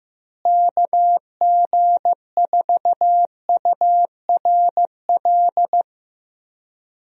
Morse code, heard as a keyed tone, about 15 wpm, 700 Hz. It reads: KG4URL
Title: USA Call Signs - 6 Characters - Repeated 2x - 15wpm